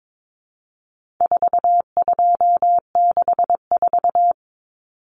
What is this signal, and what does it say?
Morse code, heard as a keyed tone, about 22 wpm, 700 Hz. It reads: 4264
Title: Numbers - 4-Digit 22wpm